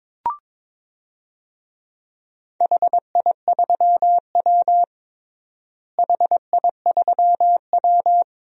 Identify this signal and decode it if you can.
Morse code, heard as a keyed tone, about 22 wpm, 700 Hz. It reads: HI3W HI3W
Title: DXCC-100 Call Signs - 4 Characters - Repeated 2x - 22wpm